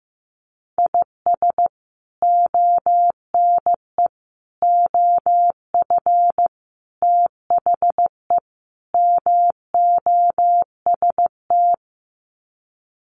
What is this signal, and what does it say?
Morse code, heard as a keyed tone, about 15 wpm, 700 Hz. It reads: IS ONE OF THE MOST
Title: Sets of 5 Words 15wpm